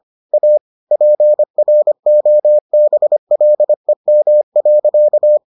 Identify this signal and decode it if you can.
Morse code, heard as a keyed tone, about 25 wpm, 600 Hz. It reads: A PROBLEM.